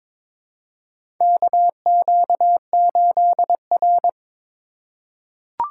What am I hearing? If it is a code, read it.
Morse code, heard as a keyed tone, about 22 wpm, 700 Hz. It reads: KQ8R
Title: USA Call Signs - 4 Characters - 2x Character Spacing - 22wpm